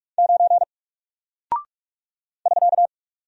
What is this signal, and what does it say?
Morse code, heard as a keyed tone, about 44 wpm, 700 Hz. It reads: 9 <SK>
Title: Single Character (Letter:Number:Punctuation:Prosign) 45wpm